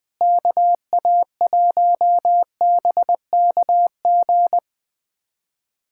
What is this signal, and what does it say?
Morse code, heard as a keyed tone, about 20 wpm, 700 Hz. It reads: KA1BKG
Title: USA Callsign DE Callsign - 6 Characters - 20wpm